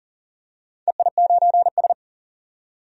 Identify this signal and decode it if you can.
Morse code, heard as a keyed tone, about 40 wpm, 700 Hz. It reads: EI9S
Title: DXCC-100 Call Signs - 4 Characters - 40wpm